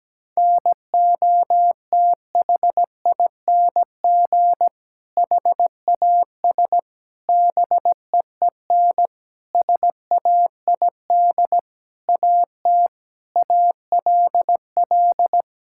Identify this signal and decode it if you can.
Morse code, heard as a keyed tone, about 17 wpm, 700 Hz. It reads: NOTHING HAS BEEN SAID AT ALL
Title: Sentences from Top 1000 Words - Review Part 5 - 17wpm